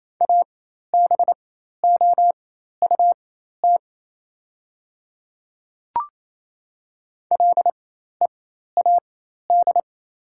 Morse code, heard as a keyed tone, about 28 wpm, spaced as for 13 wpm, 700 Hz. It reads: ABOUT LEAD
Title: 4x Character Spacing - Top 500 Words - 28wpm